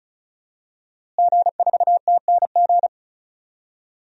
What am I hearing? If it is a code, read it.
Morse code, heard as a keyed tone, about 36 wpm, 700 Hz. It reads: G4TNG
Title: DXCC-100 Call Signs - 5 Characters - Repeated 2x - 35wpm